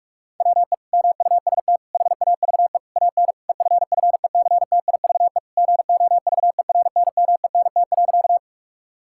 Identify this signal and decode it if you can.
Morse code, heard as a keyed tone, about 42 wpm, 700 Hz. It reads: WE MUST HAVE AN EFFECTIVE GOVERNMENT.
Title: Sentences from Top 1000 Words - Review Part 6 - 45wpm